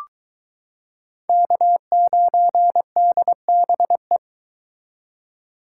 Morse code, heard as a keyed tone, about 22 wpm, 700 Hz. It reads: K9DBE